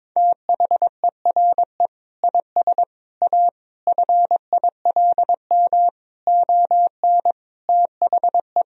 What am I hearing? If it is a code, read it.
Morse code, heard as a keyed tone, about 22 wpm, 700 Hz. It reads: THERE IS A FILM ON THE
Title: Sentences from Top 1000 Words - Review Part 3 - 22wpm